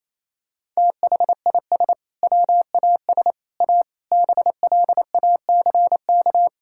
Morse code, heard as a keyed tone, about 28 wpm, 700 Hz. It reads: THIS WAS A BLACK
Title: Sentences from Top 1600 Words 28wpm